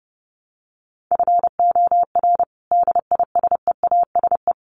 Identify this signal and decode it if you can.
Morse code, heard as a keyed tone, about 30 wpm, 700 Hz. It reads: FOR DISEASE
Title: Sets of 5 Words Encore2 30wpm